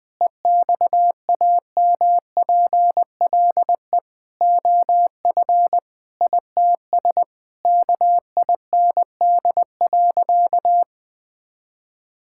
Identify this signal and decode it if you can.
Morse code, heard as a keyed tone, about 21 wpm, 700 Hz. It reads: EXAMPLE OF ITS KIND.
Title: Sentences with 6-Word Phrases - 20wpm